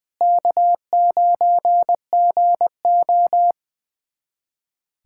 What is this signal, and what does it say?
Morse code, heard as a keyed tone, about 21 wpm, 700 Hz. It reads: K9GO
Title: USA Callsign DE Callsign - 4 Characters - 20wpm